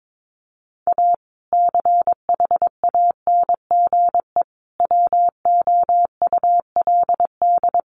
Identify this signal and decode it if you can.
Morse code, heard as a keyed tone, about 22 wpm, 700 Hz. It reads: A CHANGE WOULD
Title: Sentences from Top 1400 Words 22wpm